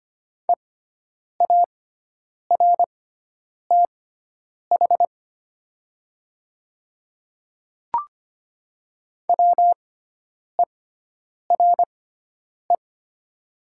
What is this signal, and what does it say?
Morse code, heard as a keyed tone, about 25 wpm, spaced as for 9 wpm, 700 Hz. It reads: EARTH WERE
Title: 6x Character Spacing - Top 500 Words - 25wpm